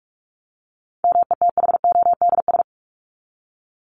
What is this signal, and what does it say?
Morse code, heard as a keyed tone, about 44 wpm, 700 Hz. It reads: METHODS
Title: ES:S Suffix - Top 100 Words with Base Words 45wpm